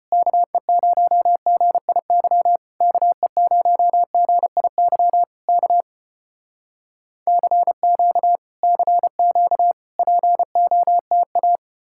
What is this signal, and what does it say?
Morse code, heard as a keyed tone, about 34 wpm, 700 Hz. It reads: KE0GIY KE0GIY K CQ CQ POTA